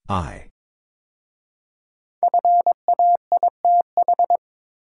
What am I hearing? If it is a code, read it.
Morse code, heard as a keyed tone, about 22 wpm, 700 Hz. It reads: FAITH
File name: 5 Letter Words - First 3 Letters Given - 22wpm